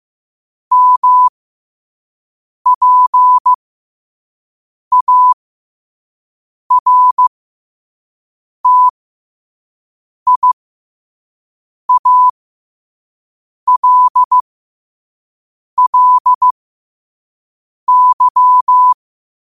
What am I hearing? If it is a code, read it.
Morse code, heard as a keyed tone, about 15 wpm, spaced as for 5 wpm, 1000 Hz. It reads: MPARTIALLY
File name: qotdfiles / 5wpm